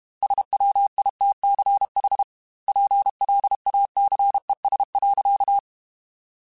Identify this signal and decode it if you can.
Morse code, heard as a keyed tone, about 32 wpm, 800 Hz. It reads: SWITCH PLACES.